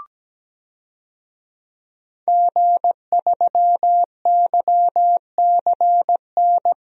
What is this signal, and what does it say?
Morse code, heard as a keyed tone, about 17 wpm, 700 Hz. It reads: G3YCN
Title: DXCC-100 Call Signs - 5 Characters - 17wpm